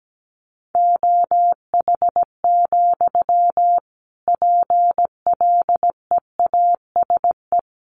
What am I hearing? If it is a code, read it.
Morse code, heard as a keyed tone, about 17 wpm, 700 Hz. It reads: OH, PLEASE
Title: Sentences from Top 1500 Words 17wpm